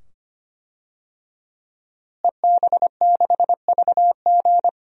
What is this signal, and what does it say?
Morse code, heard as a keyed tone, about 25 wpm, 700 Hz. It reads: EB6VG
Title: DXCC-100 Call Signs - 5 Characters - Repeated 2x - 25wpm